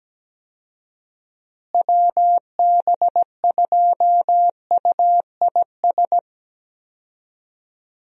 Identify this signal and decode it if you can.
Morse code, heard as a keyed tone, about 17 wpm, 700 Hz. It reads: WB2UIS
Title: USA Call Signs - 6 Characters - Repeated 2x - 17wpm